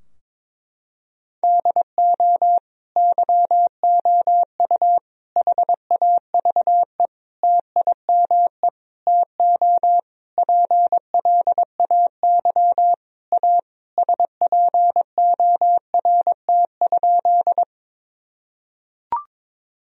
Morse code, heard as a keyed tone, about 21 wpm, 700 Hz. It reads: DO YOU HAVE TIME TO PLAY A SPORT?
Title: Sentences from Top 1000 Words - Review Part 4 - 22wpm